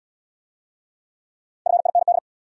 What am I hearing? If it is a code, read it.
Morse code, heard as a keyed tone, about 69 wpm, 700 Hz. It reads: FIND